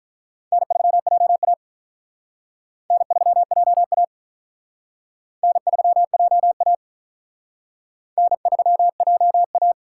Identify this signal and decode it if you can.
Morse code, heard as a keyed tone, about 42 wpm, 700 Hz. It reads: N3JA N3JA N3JA N3JA
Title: Speed Racing - USA 1x2 Call Signs 35wpm